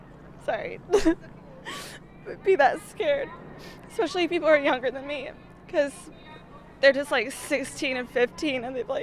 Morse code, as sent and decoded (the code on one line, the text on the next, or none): none